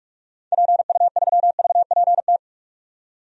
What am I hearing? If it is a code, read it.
Morse code, heard as a keyed tone, about 44 wpm, 700 Hz. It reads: PU3VPT
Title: DXCC-100 Call Signs - 6 Characters - 45wpm